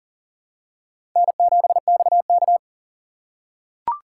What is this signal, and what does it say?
Morse code, heard as a keyed tone, about 40 wpm, 700 Hz. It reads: N7XK
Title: USA Call Signs - 4x Character Spacing - 40wpm